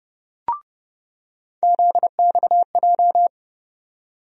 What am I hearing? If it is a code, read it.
Morse code, heard as a keyed tone, about 30 wpm, 700 Hz. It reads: ZXJ
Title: USA Call Sign Postfixes - 3 Letters 30wpm